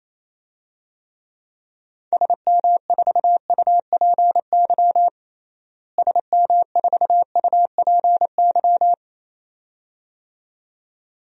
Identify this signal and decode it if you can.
Morse code, heard as a keyed tone, about 28 wpm, 700 Hz. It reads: SM4UPY SM4UPY
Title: DXCC-100 Call Signs - 6 Characters - Repeated 2x - 28wpm